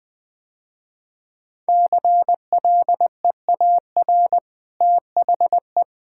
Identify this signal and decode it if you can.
Morse code, heard as a keyed tone, about 21 wpm, 700 Hz. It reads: CLEAR THE
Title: Easy Sentences with Hints - 20wpm